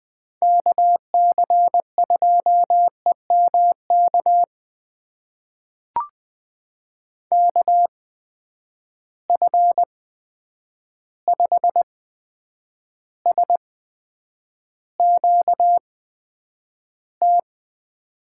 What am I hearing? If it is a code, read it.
Morse code, heard as a keyed tone, about 20 wpm, spaced as for 9 wpm, 700 Hz. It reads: KC2EMK KF5SQT
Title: USA Call Signs - 6 Characters - 8x Character Spacing - 20wpm